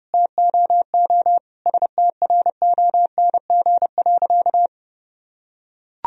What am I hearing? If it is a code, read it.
Morse code, heard as a keyed tone, about 32 wpm, 700 Hz. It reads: TOO STRONG.
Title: Sentences from Top 1000 Words - Review Part 4 - 30wpm